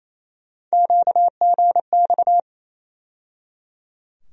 Morse code, heard as a keyed tone, about 28 wpm, 700 Hz. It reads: QGX
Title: USA Call Sign Postfixes - 3 Letters 28wpm